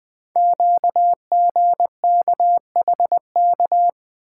Morse code, heard as a keyed tone, about 20 wpm, 700 Hz. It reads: QGKHK
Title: Code Groups - 5 Letters - 20wpm